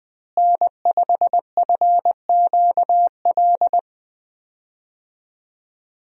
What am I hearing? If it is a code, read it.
Morse code, heard as a keyed tone, about 20 wpm, 700 Hz. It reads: N5FQL
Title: USA Call Signs - 5 Characters - Repeated 2x - 20wpm